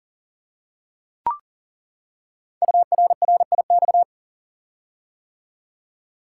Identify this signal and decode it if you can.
Morse code, heard as a keyed tone, about 42 wpm, 690 Hz. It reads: URRIX
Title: Code Groups - 5 Letters - 40wpm